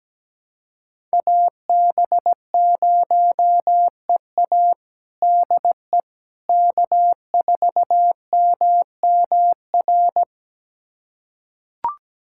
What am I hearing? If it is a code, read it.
Morse code, heard as a keyed tone, about 17 wpm, 700 Hz. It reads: AB0EA DE K4MMR
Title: USA Callsign DE Callsign - 5 Characters - 17wpm